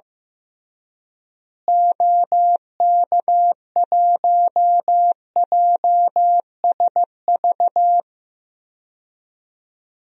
Morse code, heard as a keyed tone, about 16 wpm, 700 Hz. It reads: OK1JSV
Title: DXCC-100 Call Signs - 6 Characters - Repeated 2x - 15wpm